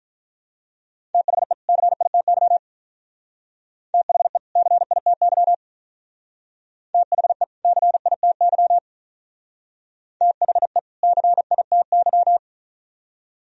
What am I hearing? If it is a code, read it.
Morse code, heard as a keyed tone, about 45 wpm, 700 Hz. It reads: THE CITY THE CITY THE CITY THE CITY